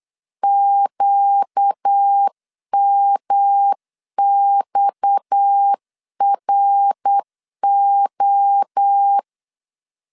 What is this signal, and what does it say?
Morse code, heard as a keyed tone, about 8 wpm, 800 Hz. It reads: QMXRO
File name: LEZIONI / 9 286 8wpm 3425